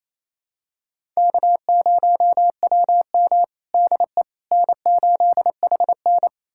Morse code, heard as a keyed tone, about 27 wpm, 700 Hz. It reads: K0WM DE N8HN
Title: USA Callsign DE Callsign - 4 Characters - 28wpm